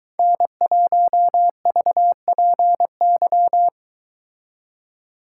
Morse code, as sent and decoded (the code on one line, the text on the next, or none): -. .---- ...- .--. -.--
N1VPY